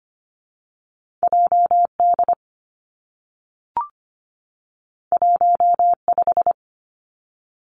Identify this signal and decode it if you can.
Morse code, heard as a keyed tone, about 25 wpm, 700 Hz. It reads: J D 1 5
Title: Two Challenging Letter-Number Combinations 25wpm